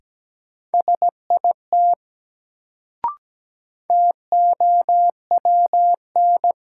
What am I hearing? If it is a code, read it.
Morse code, heard as a keyed tone, about 17 wpm, 700 Hz. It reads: SIT TOWN
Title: Sentences from Top 500 Words - New Word Drill - 17wpm